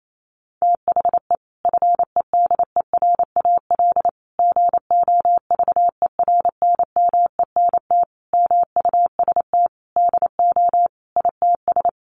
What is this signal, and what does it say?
Morse code, heard as a keyed tone, about 28 wpm, 700 Hz. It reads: THE FEDERAL GOVERNMENT MUST DO ITS
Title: Sentences from Top 1100 Words 28wpm